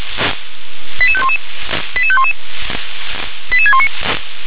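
Five-tone selective call (ZVEI), heard as quick runs of five tones, tones of about 70 ms. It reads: 88511 88511 88619
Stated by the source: ZVEI five-tone call 88731 received by SDR.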